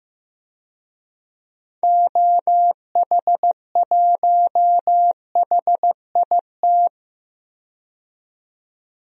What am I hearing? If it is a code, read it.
Morse code, heard as a keyed tone, about 15 wpm, 700 Hz. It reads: OH1HIT
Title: DXCC-100 Call Signs - 6 Characters - 15wpm